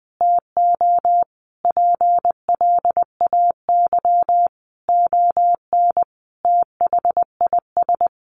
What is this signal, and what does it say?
Morse code, heard as a keyed tone, about 20 wpm, 700 Hz. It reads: TO PLAY ON THIS
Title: Sentences from Top 1500 Words 20wpm